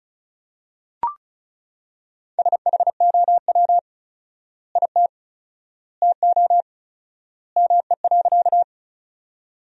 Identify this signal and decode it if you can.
Morse code, heard as a keyed tone, about 36 wpm, 700 Hz. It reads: SHOW IT TO ME.
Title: 4x Word-Spacing - Sentences from Top 200 Words - 35wpm